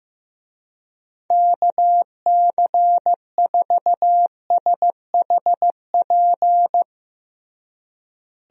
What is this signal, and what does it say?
Morse code, heard as a keyed tone, about 15 wpm, 700 Hz. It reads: KC4SHP
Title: USA Call Signs - 6 Characters - Repeated 3x - 15wpm